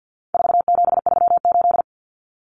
Morse code, heard as a keyed tone, about 50 wpm, 700 Hz. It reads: 4738
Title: Numbers - 4-Digit 50wpm